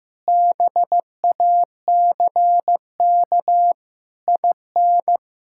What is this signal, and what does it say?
Morse code, heard as a keyed tone, about 15 wpm, 700 Hz. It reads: BACK IN
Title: Pangram Challenge v2 - 15wpm